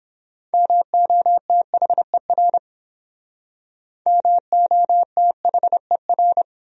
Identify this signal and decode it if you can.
Morse code, heard as a keyed tone, about 30 wpm, 700 Hz. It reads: MOTHER MOTHER